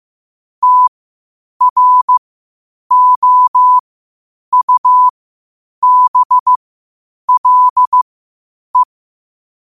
Morse code, heard as a keyed tone, about 15 wpm, spaced as for 8 wpm, 1000 Hz. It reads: TROUBLE
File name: qotdfiles / 8wpm